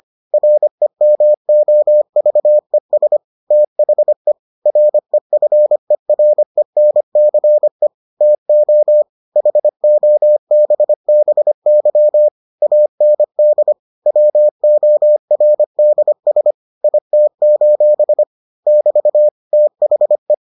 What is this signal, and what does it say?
Morse code, heard as a keyed tone, about 25 wpm, 600 Hz. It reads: REMOVES THE REFERENCE TO HOBBY AND WORDS IT: <BT> THE